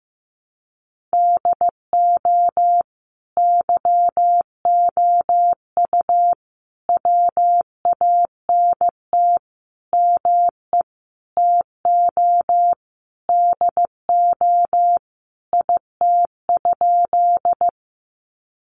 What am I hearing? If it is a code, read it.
Morse code, heard as a keyed tone, about 15 wpm, 700 Hz. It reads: DO YOU WANT ME TO DO IT?
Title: Sentences from Top 200 Words 15wpm